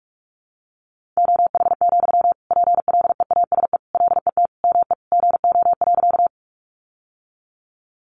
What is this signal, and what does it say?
Morse code, heard as a keyed tone, about 44 wpm, 700 Hz. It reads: OH, PLEASE LET ME GO.
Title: Sentences from Top 1500 Words 45wpm